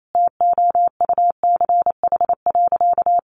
Morse code, heard as a keyed tone, about 28 wpm, 700 Hz. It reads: TOUCH.
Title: Sentences from Top 1500 Words 28wpm